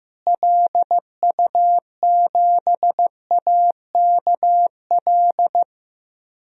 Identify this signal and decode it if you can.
Morse code, heard as a keyed tone, about 15 wpm, 700 Hz. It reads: LU7AKL
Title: DXCC-100 Call Signs - 6 Characters - Repeated 3x - 15wpm